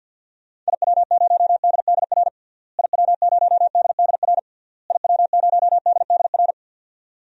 Morse code, heard as a keyed tone, about 50 wpm, 700 Hz. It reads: IW0DDR IW0DDR IW0DDR
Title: DXCC-100 Call Signs - 6 Characters - Repeated 3x - 50wpm